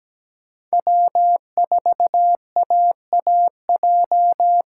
Morse code, heard as a keyed tone, about 20 wpm, 700 Hz. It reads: W4AAJ
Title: Speed Racing - USA 1x3 Call Signs 17wpm